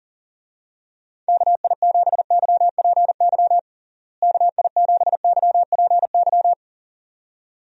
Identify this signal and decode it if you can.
Morse code, heard as a keyed tone, about 40 wpm, 700 Hz. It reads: KI7YPY KI7YPY
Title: USA Call Signs - 6 Characters - Repeated 2x - 40wpm